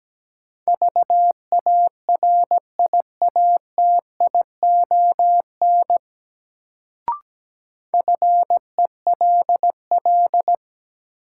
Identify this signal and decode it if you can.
Morse code, heard as a keyed tone, about 17 wpm, 700 Hz. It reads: VARIATION FELL
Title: Sentences from Top 1600 Words - New Word Drill - 17wpm